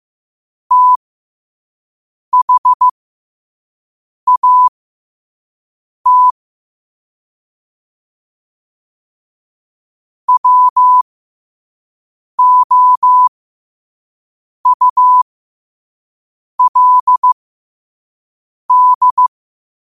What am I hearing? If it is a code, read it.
Morse code, heard as a keyed tone, about 15 wpm, spaced as for 5 wpm, 1000 Hz. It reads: THAT WOULD
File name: 5wpm